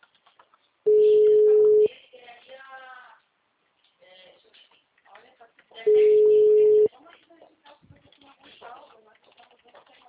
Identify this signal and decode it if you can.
Ringback tone (Europe, 425 Hz), with speech.